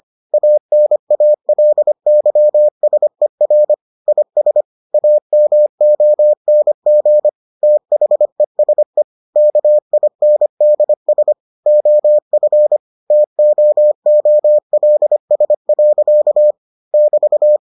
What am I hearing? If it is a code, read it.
Morse code, heard as a keyed tone, about 25 wpm, 600 Hz. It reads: ANALYSER IS AMONG THESE KINDS OF TOOLS. <BT>